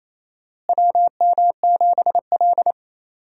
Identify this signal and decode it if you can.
Morse code, heard as a keyed tone, about 28 wpm, 700 Hz. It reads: WM7L
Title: USA Call Signs - 4 Characters - Repeated 3x - 28wpm